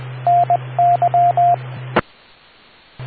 Morse code, heard as a keyed tone, about 20 wpm, 700 Hz, with radio noise. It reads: NY